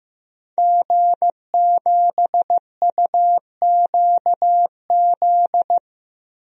Morse code, heard as a keyed tone, about 15 wpm, 700 Hz. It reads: G7UQZ
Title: DXCC-100 Call Signs - 5 Characters - Repeated 2x - 15wpm